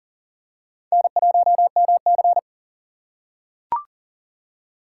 Morse code, heard as a keyed tone, about 40 wpm, 700 Hz. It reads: N1MC
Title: USA Call Signs - 4 Characters - 4x Character Spacing - 40wpm